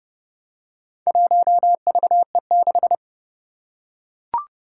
Morse code, heard as a keyed tone, about 30 wpm, 700 Hz. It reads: 1VE6
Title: Four Challenging Letter-Number Combinations 30wpm